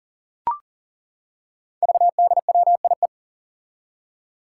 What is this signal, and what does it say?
Morse code, heard as a keyed tone, about 39 wpm, 700 Hz. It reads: VDWIE
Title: Five Challenging Letter-Number Combinations 40wpm